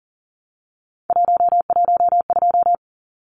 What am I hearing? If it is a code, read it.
Morse code, heard as a keyed tone, about 40 wpm, 700 Hz. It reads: 112
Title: Numbers - 3-Digit 40wpm